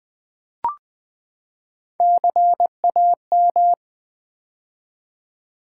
Morse code, heard as a keyed tone, about 20 wpm, 700 Hz. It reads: CAM
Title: Build A Word - 3 Letters At A Time - 6-Letter Words - 20wpm